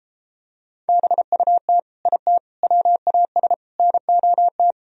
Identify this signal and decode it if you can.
Morse code, heard as a keyed tone, about 33 wpm, 700 Hz. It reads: BUT IT WAS NOT